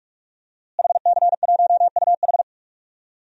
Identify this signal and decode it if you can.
Morse code, heard as a keyed tone, about 45 wpm, 700 Hz. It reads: HC1UH
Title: DXCC-100 Call Signs - 5 Characters - Repeated 3x - 45wpm